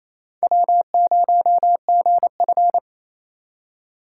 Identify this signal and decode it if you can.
Morse code, heard as a keyed tone, about 28 wpm, 700 Hz. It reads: W0GF
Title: USA Call Signs - 4 Characters - 3x Character Spacing - 28wpm